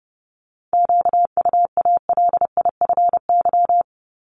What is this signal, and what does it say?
Morse code, heard as a keyed tone, about 28 wpm, 700 Hz. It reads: QUALIFY
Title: Top 1751-2000 Words 30wpm